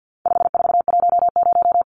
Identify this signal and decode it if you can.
Morse code, heard as a keyed tone, about 50 wpm, 700 Hz. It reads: 5410
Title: Numbers - 4-Digit 50wpm